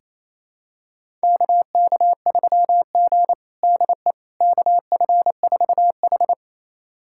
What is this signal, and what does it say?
Morse code, heard as a keyed tone, about 28 wpm, 700 Hz. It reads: KK3G DE KF4H
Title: USA Callsign DE Callsign - 4 Characters - 28wpm